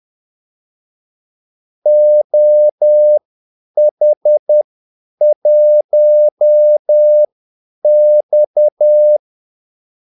Morse code, heard as a keyed tone, about 10 wpm, spaced as for 8 wpm, 600 Hz. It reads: OH1X
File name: Callsigns-10-8wpm